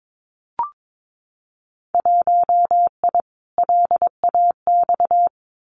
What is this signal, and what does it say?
Morse code, heard as a keyed tone, about 22 wpm, 700 Hz. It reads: 1I LAX
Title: Winter Field Day Exchanges 22wpm